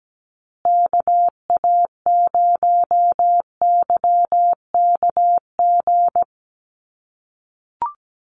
Morse code, heard as a keyed tone, about 17 wpm, 700 Hz. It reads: KA0YKG
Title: USA Call Signs Repeated 3x 17wpm